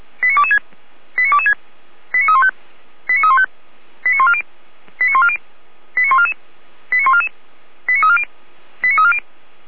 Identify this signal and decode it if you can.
Five-tone selective call (ZVEI), heard as quick runs of five tones, tones of about 80 ms. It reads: 89226 89226 89315 89315 89140 89140 89144 89144 89340 89340